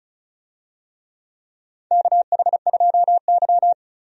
Morse code, heard as a keyed tone, about 34 wpm, 700 Hz. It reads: KH2Y
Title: USA Call Signs - 4 Characters - 35wpm